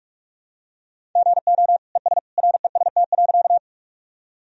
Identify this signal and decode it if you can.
Morse code, heard as a keyed tone, about 45 wpm, 700 Hz. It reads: GO ES REST.
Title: Sentences from Top 500 Words and Abbreviations 45wpm